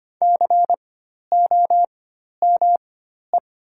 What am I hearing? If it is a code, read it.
Morse code, heard as a keyed tone, about 25 wpm, spaced as for 12 wpm, 700 Hz. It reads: COME
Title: 4x Character Spacing - Top 100 Words - 25wpm